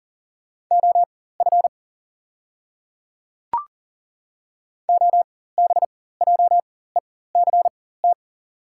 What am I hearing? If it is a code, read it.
Morse code, heard as a keyed tone, about 40 wpm, spaced as for 19 wpm, 690 Hz. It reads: OF OBJECT
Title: 4x Character Spacing - Top 500 Words - 40wpm